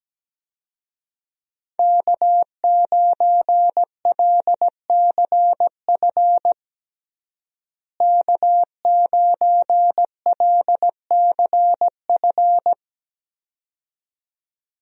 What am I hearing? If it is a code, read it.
Morse code, heard as a keyed tone, about 17 wpm, 700 Hz. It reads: K9LCF K9LCF
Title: USA Call Signs - 5 Characters - Repeated 2x - 17wpm